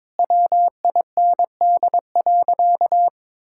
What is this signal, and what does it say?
Morse code, heard as a keyed tone, about 22 wpm, 700 Hz. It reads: WIND.
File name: practice / 22wpm / Sentences with 3-Word Phrases - 22wpm